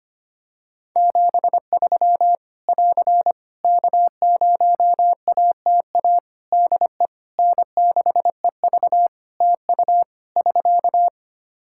Morse code, heard as a keyed tone, about 25 wpm, 700 Hz. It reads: 73 <AR> K0ATA DE N6EV TU <SK>
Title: Practice QSO - Exchange 4 - Ending - 25wpm